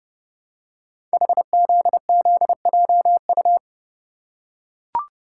Morse code, heard as a keyed tone, about 28 wpm, 700 Hz. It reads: HZZJU
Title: Code Groups - 5 Letters - 30wpm